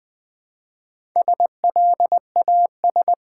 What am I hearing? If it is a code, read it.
Morse code, heard as a keyed tone, about 20 wpm, 700 Hz. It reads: SLAS
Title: Build A Word - 1 Letter At A Time - 5-Letter Words - 20wpm